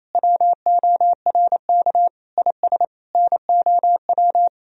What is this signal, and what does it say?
Morse code, heard as a keyed tone, about 27 wpm, 700 Hz. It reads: WORK IS NOW
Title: Sentences from Top 400 Words and Abbreviations 28wpm